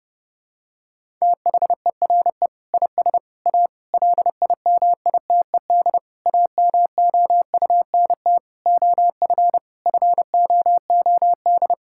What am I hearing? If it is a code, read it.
Morse code, heard as a keyed tone, about 30 wpm, 700 Hz. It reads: THERE IS A LIMITED AMOUNT OF FOOD
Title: Sentences from Top 1000 Words - Review Part 6 - 30wpm